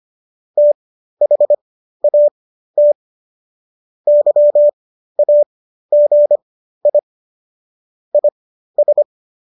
Morse code, heard as a keyed tone, about 25 wpm, spaced as for 13 wpm, 590 Hz. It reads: THAT YAGI IS